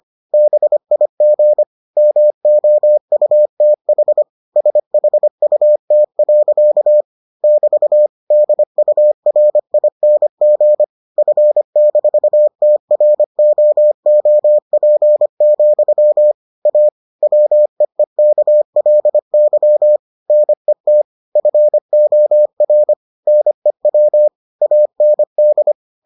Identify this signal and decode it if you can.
Morse code, heard as a keyed tone, about 25 wpm, 600 Hz. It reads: BIG MOUTH SHUT. <BT> DURING F-TROOP, A WEEKLY NET FOR NEW AND